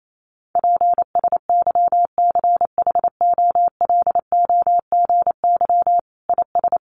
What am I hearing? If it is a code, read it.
Morse code, heard as a keyed tone, about 28 wpm, 700 Hz. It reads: PSYCHOLOGY IS
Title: Sentences from Top 1400 Words 28wpm